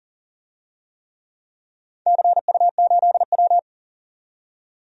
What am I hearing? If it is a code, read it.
Morse code, heard as a keyed tone, about 40 wpm, 700 Hz. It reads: CU8W
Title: DXCC-100 Call Signs - 4 Characters - 40wpm